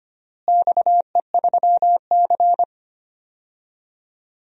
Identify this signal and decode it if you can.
Morse code, heard as a keyed tone, about 25 wpm, 700 Hz. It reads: XE3C
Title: DXCC-100 Call Signs - 4 Characters - Repeated 2x - 25wpm